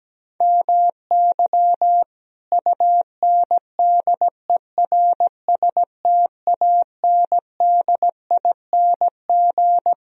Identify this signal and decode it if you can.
Morse code, heard as a keyed tone, about 17 wpm, 700 Hz. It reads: MY UNDERSTANDING
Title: Sentences from Top 1000 Words - Review Part 3 - 17wpm